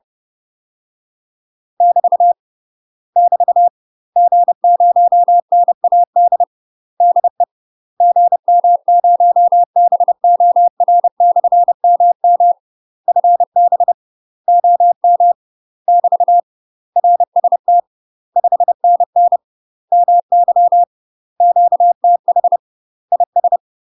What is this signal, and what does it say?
Morse code, heard as a keyed tone, about 30 wpm, 700 Hz. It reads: <BT> <BT> G0NAD DE GM0BOR/MM FB OM <BT> RST 5NN MY QTH IS
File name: UK MASTER QSOs TRAINING / 30WPM / UK MASTER QSOs 0017